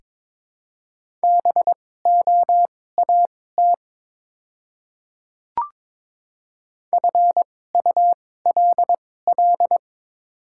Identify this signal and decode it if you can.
Morse code, heard as a keyed tone, about 22 wpm, spaced as for 16 wpm, 700 Hz. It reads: BOAT FULL
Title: 2x Character Spacing - Top 500 Words - 22wpm